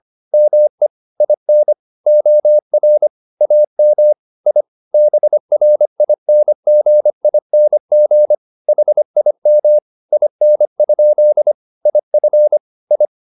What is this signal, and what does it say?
Morse code, heard as a keyed tone, about 25 wpm, 600 Hz. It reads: ME IN OR AM I BRINGING HIM IN? IF I